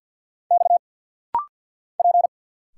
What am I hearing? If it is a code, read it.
Morse code, heard as a keyed tone, about 50 wpm, 690 Hz. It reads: X P
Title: Single Letters - Rapid-Fire 50wpm